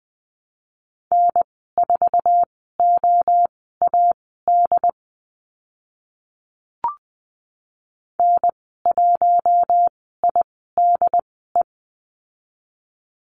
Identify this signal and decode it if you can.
Morse code, heard as a keyed tone, about 20 wpm, spaced as for 14 wpm, 700 Hz. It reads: N4OAD N1IDE
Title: 076 Lesson 13g - Letter L - Call Signs 20wpm